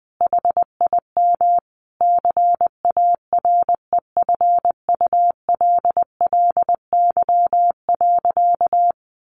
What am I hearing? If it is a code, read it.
Morse code, heard as a keyed tone, about 20 wpm, 700 Hz. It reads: HIM CAREFULLY.
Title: Sentences from Top 1500 Words 20wpm